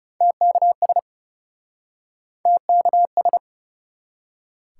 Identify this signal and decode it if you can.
Morse code, heard as a keyed tone, about 41 wpm, 700 Hz. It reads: TKS TKS